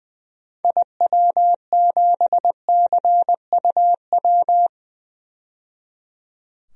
Morse code, heard as a keyed tone, about 20 wpm, 700 Hz. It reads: IW7CUW
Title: DXCC-100 Call Signs - 6 Characters - 20wpm